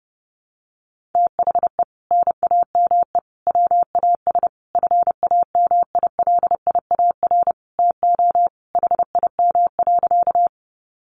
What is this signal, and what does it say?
Morse code, heard as a keyed tone, about 30 wpm, 700 Hz. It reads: THE NAME WAS FAMILIAR TO HIM.
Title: Sentences from Top 1000 Words 30wpm